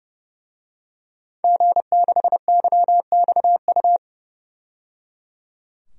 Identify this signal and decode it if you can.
Morse code, heard as a keyed tone, about 29 wpm, 700 Hz. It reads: G6YXU
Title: DXCC-100 Call Signs - 5 Characters - 30wpm